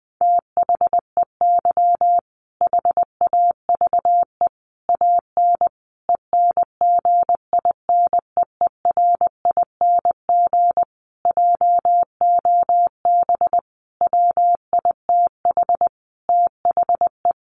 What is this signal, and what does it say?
Morse code, heard as a keyed tone, about 19 wpm, 700 Hz. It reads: THEY HAVE AN ENGINEERING JOB WITH THE
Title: Sentences from Top 1500 Words 20wpm